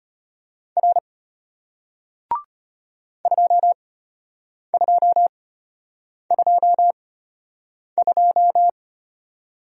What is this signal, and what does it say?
Morse code, heard as a keyed tone, about 33 wpm, spaced as for 8 wpm, 700 Hz. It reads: R 2222